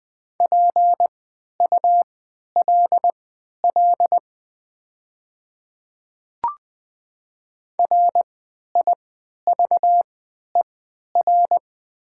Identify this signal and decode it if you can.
Morse code, heard as a keyed tone, about 20 wpm, spaced as for 11 wpm, 700 Hz. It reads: PULL RIVER
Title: 3x Character Spacing - Top 500 Words - 20wpm